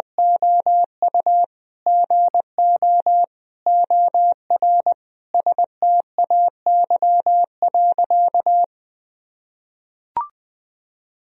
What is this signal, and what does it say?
Morse code, heard as a keyed tone, about 20 wpm, 700 Hz. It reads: OU GO OR STAY.
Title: Sentences from Top 1000 Words - Review Part 3 - 20wpm